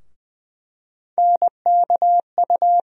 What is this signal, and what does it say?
Morse code, heard as a keyed tone, about 20 wpm, 700 Hz. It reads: NKU
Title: USA Call Sign Postfixes - 3 Letters 20wpm